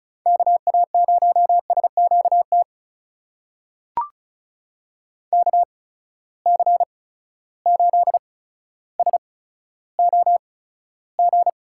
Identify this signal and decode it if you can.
Morse code, heard as a keyed tone, about 36 wpm, spaced as for 15 wpm, 700 Hz. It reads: KA0SQT KC8SOG
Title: USA Call Signs - 6 Characters - 8x Character Spacing - 35wpm